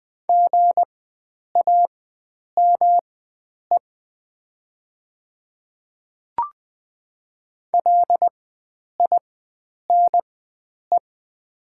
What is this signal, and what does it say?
Morse code, heard as a keyed tone, about 20 wpm, spaced as for 9 wpm, 700 Hz. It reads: GAME LINE
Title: 4x Character Spacing - Top 500 Words - 20wpm